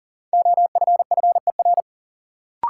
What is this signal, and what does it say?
Morse code, heard as a keyed tone, about 40 wpm, 700 Hz. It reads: OFFER
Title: Sentences from Top 900 Words - New Word Drill - 40wpm